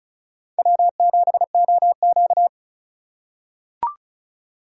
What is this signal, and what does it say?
Morse code, heard as a keyed tone, about 36 wpm, 700 Hz. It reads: W7OQ
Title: K1USN SST QSO Practice - 35wpm